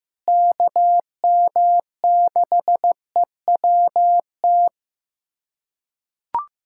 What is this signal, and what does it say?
Morse code, heard as a keyed tone, about 15 wpm, 700 Hz. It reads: KM6EWT
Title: USA Callsign DE Callsign - 6 Characters - 15wpm